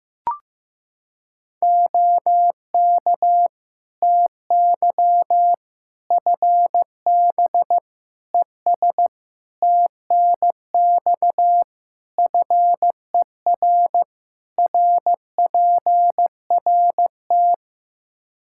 Morse code, heard as a keyed tone, about 15 wpm, 700 Hz. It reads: OK TY FB ES TNX FER RPRT
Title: K1USN SST QSO Practice 15wpm